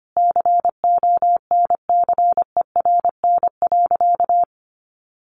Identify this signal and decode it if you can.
Morse code, heard as a keyed tone, about 25 wpm, 700 Hz. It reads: CONCERN.
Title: Sentences from Top 1400 Words 25wpm